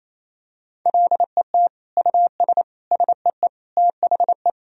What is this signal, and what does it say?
Morse code, heard as a keyed tone, about 28 wpm, 700 Hz. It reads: LET US SEE THE
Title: Sentences from Top 1000 Words - Review Part 2 - 28wpm